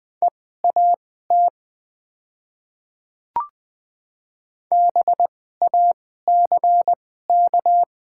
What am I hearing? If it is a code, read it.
Morse code, heard as a keyed tone, about 20 wpm, spaced as for 14 wpm, 700 Hz. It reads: EAT BACK
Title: 2x Character Spacing - Top 300 Words - 20wpm